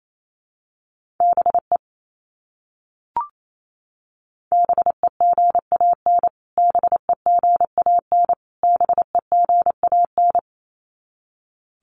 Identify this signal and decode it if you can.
Morse code, heard as a keyed tone, about 28 wpm, 700 Hz. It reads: BE BEGAN BEGAN BEGAN
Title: Top 500 Words Repeated 28wpm